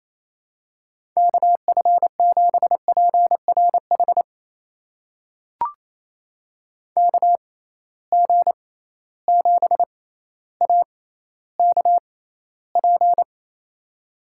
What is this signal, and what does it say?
Morse code, heard as a keyed tone, about 28 wpm, spaced as for 14 wpm, 700 Hz. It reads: KF7PRH KG7AKP
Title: USA Call Signs - 6 Characters - 6x Character Spacing - 28wpm